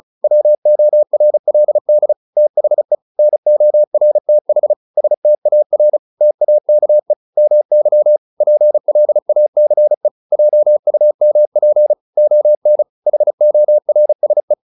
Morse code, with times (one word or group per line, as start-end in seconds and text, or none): 0.2-2.1 WORLD
2.4-3.0 THE
3.2-4.7 NORTH
5.0-6.0 STAR
6.2-7.1 TAKE
7.4-8.2 MY
8.4-10.1 PLACE
10.3-11.9 JUMP
12.2-12.8 ON
13.1-14.5 HORSE